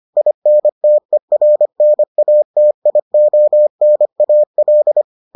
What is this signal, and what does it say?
Morse code, heard as a keyed tone, about 25 wpm, 600 Hz. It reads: INTERNATIONAL